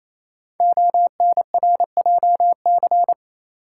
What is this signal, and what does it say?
Morse code, heard as a keyed tone, about 28 wpm, 700 Hz. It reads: ONRJC
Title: Code Groups - 5 Letters - 28wpm